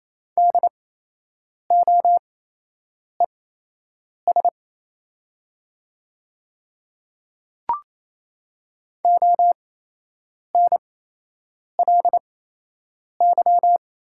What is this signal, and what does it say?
Morse code, heard as a keyed tone, about 28 wpm, spaced as for 8 wpm, 700 Hz. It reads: DOES ONLY
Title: 8x Character Spacing - Top 300 Words - 28wpm